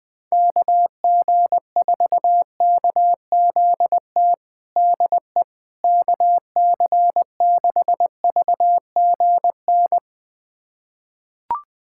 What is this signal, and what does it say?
Morse code, heard as a keyed tone, about 21 wpm, 700 Hz. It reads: KG4KZT DE KC6VGN
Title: USA Callsign DE Callsign 20wpm